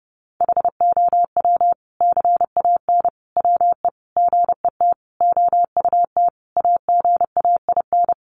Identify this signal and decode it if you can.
Morse code, heard as a keyed tone, about 30 wpm, 700 Hz. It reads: HOW CAN WE GET OUT AGAIN
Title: Sentences from Top 1200 Words 30wpm